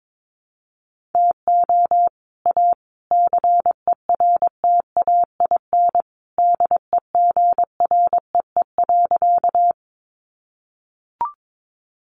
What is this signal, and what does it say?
Morse code, heard as a keyed tone, about 22 wpm, 700 Hz. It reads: TO A CERTAIN DEGREE.
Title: Sentences from Top 1500 Words 22wpm